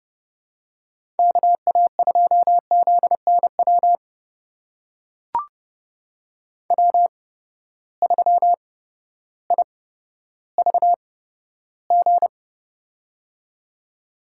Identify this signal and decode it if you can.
Morse code, heard as a keyed tone, about 32 wpm, spaced as for 14 wpm, 700 Hz. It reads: KA2ZNW W3IVG
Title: USA Call Signs - 8x Character Spacing - 30wpm